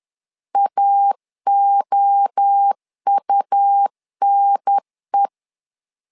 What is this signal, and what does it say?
Morse code, heard as a keyed tone, about 10 wpm, 800 Hz. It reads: AOUNE
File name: LEZIONI / 4 Q 10wpm 2210